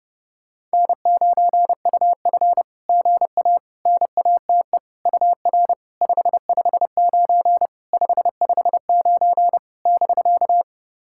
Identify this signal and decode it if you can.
Morse code, heard as a keyed tone, about 30 wpm, 690 Hz. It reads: N9UF GA NATE UR 559 559 <BK>
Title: SOTA QSO Practice 30wpm